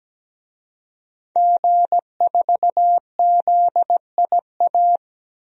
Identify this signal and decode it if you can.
Morse code, heard as a keyed tone, about 17 wpm, 700 Hz. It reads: G4ZIA
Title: DXCC-100 Call Signs - 5 Characters - Repeated 3x - 17wpm